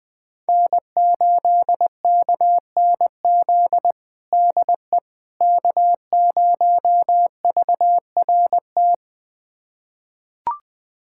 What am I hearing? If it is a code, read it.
Morse code, heard as a keyed tone, about 21 wpm, 700 Hz. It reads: N8KNZ DE K0VRT
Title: USA Callsign DE Callsign - 5 Characters - 20wpm